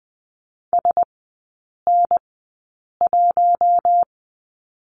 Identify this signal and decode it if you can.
Morse code, heard as a keyed tone, about 20 wpm, spaced as for 8 wpm, 700 Hz. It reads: SN1
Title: 041 Lesson 8e - Number 1 - Three Characters 20wpm